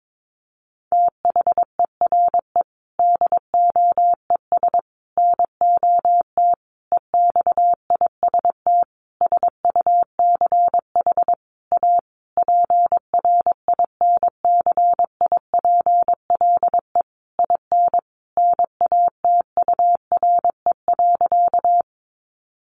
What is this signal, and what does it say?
Morse code, heard as a keyed tone, about 22 wpm, 700 Hz. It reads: THERE DOES NOT EXIST SUCH A PRINCIPLE IN NATURE.
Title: Sentences from Top 1400 Words 22wpm